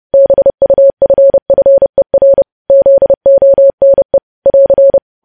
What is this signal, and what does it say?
Morse code, heard as a keyed tone, about 30 wpm, 550 Hz. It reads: BUFFER ZONE <AR>
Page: News Headlines in Morse Code at 30 WPM